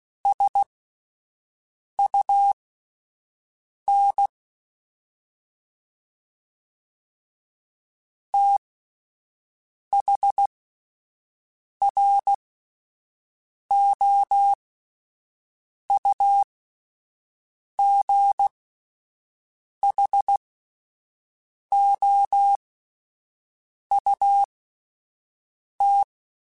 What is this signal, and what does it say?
Morse code, heard as a keyed tone, about 16 wpm, spaced as for 5 wpm, 800 Hz. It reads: SUN THROUGHOUT
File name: Spaced Characters / Spaced Characters-mp3s / Weather5 12 sp @10